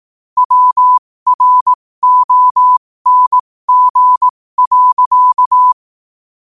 Morse code, heard as a keyed tone, about 18 wpm, spaced as for 15 wpm, 1000 Hz. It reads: WRONG.
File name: qotdfiles / 15wpm